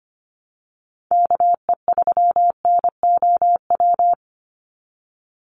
Morse code, heard as a keyed tone, about 25 wpm, 700 Hz. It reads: KE3NOW
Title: USA Call Signs 25wpm